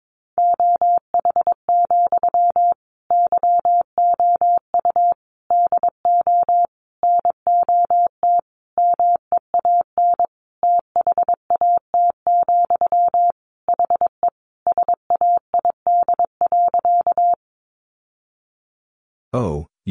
Morse code, heard as a keyed tone, about 22 wpm, 700 Hz. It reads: OH, YOU DO NOT MEAN THAT, HE SAID.
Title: Sentences from Top 600 Words 22wpm